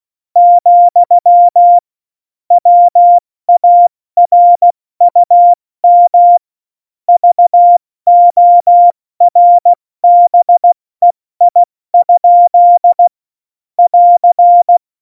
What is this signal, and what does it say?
Morse code, heard as a keyed tone, about 16 wpm, spaced as for 14 wpm, 700 Hz. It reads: , WARUM VORBEI? <AR>
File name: Morsekurs / Witze / Witz 11. Februar 2020, 70 F80